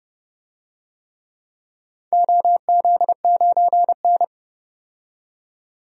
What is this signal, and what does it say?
Morse code, heard as a keyed tone, about 30 wpm, 700 Hz. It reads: OZ9N
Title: DXCC-100 Call Signs - 4 Characters - 30wpm